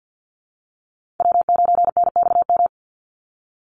Morse code, heard as a keyed tone, about 49 wpm, 700 Hz. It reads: W9NXM
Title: Call Signs Repeated 3x - 50wpm